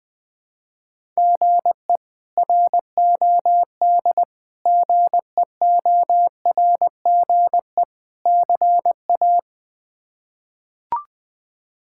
Morse code, heard as a keyed tone, about 21 wpm, 700 Hz. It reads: GE ROD GEORGE CA
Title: K1USN SST QSO Practice - 20wpm